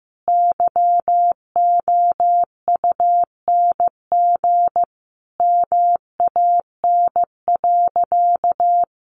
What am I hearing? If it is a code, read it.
Morse code, heard as a keyed tone, about 15 wpm, 700 Hz. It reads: YOUNG MAN.
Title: Sentences from Top 1000 Words 15wpm